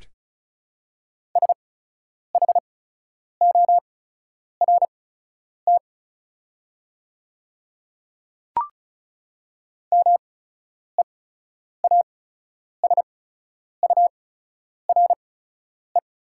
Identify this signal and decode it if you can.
Morse code, heard as a keyed tone, about 34 wpm, spaced as for 10 wpm, 690 Hz. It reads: SHORT MEASURE
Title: 8x Character Spacing - Top 500 Words - 35wpm